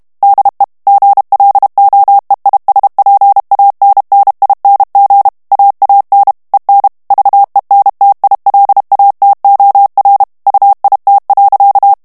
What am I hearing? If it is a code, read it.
Morse code, heard as a keyed tone, about 32 wpm, 800 Hz. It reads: DE GLOEISPANNING AAN EN VENTILATOR UIT.
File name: HOLLANDAIS texte clair 2 vitesse 30 mots minute